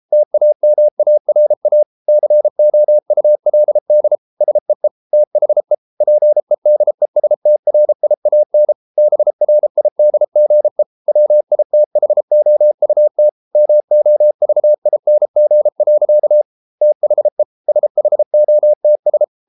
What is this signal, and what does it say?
Morse code, heard as a keyed tone, about 33 wpm, 600 Hz. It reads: TAMARA COULD SEE THE PEDESTRIAN BRIDGE WITHOUT MOVING. THE SHOTS